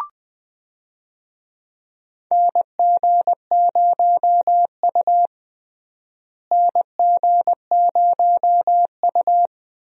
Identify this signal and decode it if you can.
Morse code, heard as a keyed tone, about 21 wpm, 700 Hz. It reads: NG0U NG0U
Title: USA Call Signs - 4 Characters - Repeated 3x - 20wpm